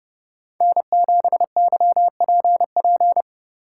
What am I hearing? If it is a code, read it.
Morse code, heard as a keyed tone, about 29 wpm, 700 Hz. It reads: N7YPP